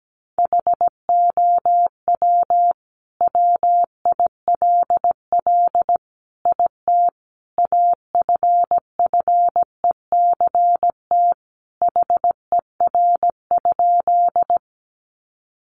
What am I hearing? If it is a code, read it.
Morse code, heard as a keyed tone, about 17 wpm, 700 Hz. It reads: HOW WILL IT AFFECT HER?
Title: Sentences from Top 900 Words 17wpm